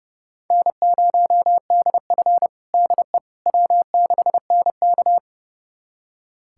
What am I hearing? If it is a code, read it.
Morse code, heard as a keyed tone, about 30 wpm, 700 Hz. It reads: N0DF DE W6NK
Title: USA Callsign DE Callsign - 4 Characters - 30wpm